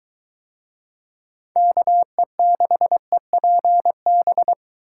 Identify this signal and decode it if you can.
Morse code, heard as a keyed tone, about 22 wpm, 700 Hz. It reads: KE6EPB